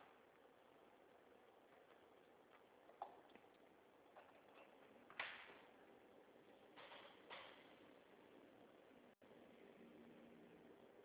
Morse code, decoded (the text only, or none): none